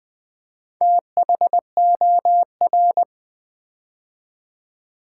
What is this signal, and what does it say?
Morse code, heard as a keyed tone, about 20 wpm, 700 Hz. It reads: THOR
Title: Build A Word - 2 Letters At A Time - 8-Letter Words - 20wpm